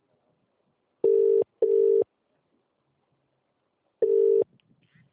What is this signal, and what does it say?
Ringback tone (UK), 400 + 450 Hz.